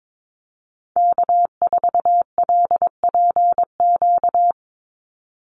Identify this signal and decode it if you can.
Morse code, heard as a keyed tone, about 22 wpm, 700 Hz. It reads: K4LPQ
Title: NA SOTA Activator Callsigns 22wpm